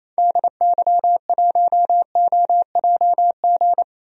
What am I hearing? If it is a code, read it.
Morse code, heard as a keyed tone, about 28 wpm, 700 Hz. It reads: DY1OJG
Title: DXCC-100 Call Signs - 6 Characters - Repeated 2x - 28wpm